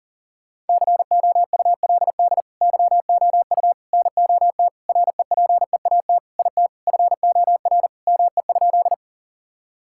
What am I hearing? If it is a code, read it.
Morse code, heard as a keyed tone, about 40 wpm, 700 Hz. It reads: COULD YOU NOT REPEAT IT FOR ME?
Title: Sentences from Top 1600 Words 40wpm